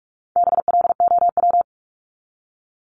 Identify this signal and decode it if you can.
Morse code, heard as a keyed tone, about 44 wpm, 700 Hz. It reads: BLOW
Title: Top 1751-2000 Words 45wpm